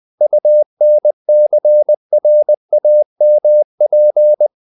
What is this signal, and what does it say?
Morse code, heard as a keyed tone, about 20 wpm, 600 Hz. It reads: UNCRAMP
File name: the old man and the sea / the old man and the sea part 1 20wpm